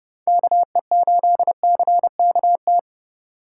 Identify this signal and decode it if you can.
Morse code, heard as a keyed tone, about 32 wpm, 700 Hz. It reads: KE8CKT